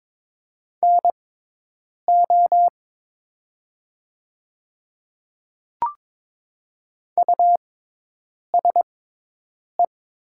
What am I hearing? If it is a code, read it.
Morse code, heard as a keyed tone, about 22 wpm, spaced as for 8 wpm, 690 Hz. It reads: NO USE
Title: 6x Character Spacing - Top 100 Words - 22wpm